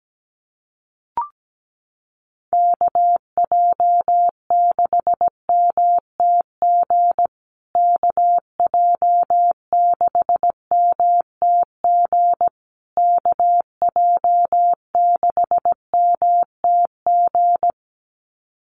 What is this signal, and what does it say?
Morse code, heard as a keyed tone, about 17 wpm, 700 Hz. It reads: KJ6MTG KJ6MTG KJ6MTG
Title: USA Call Signs Repeated 3x 17wpm